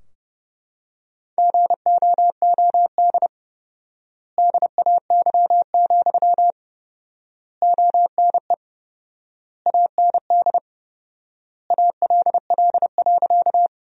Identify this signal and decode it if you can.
Morse code, heard as a keyed tone, about 31 wpm, 700 Hz. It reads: GOOD DAY, ONE AND ALL.
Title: 4x Word-Spacing - Sentences from Top 200 Words - 30wpm